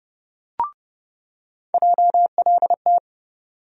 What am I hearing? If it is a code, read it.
Morse code, heard as a keyed tone, about 30 wpm, 700 Hz. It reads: JLT
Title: USA Call Sign Postfixes - 3 Letters 30wpm